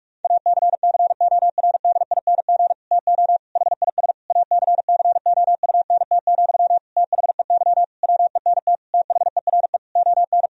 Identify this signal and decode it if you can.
Morse code, heard as a keyed tone, about 46 wpm, 700 Hz. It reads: ACCORDING TO HIS ACCOUNT, THEY WENT THERE ON
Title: Sentences from Top 1000 Words - Review Part 4 - 45wpm